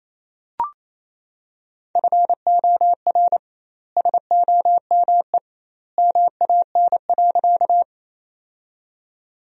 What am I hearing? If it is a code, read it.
Morse code, heard as a keyed tone, about 28 wpm, 700 Hz. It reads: FOR SOME MAN.
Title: 2x Word-Spacing - Sentences from Top 200 Words - 28wpm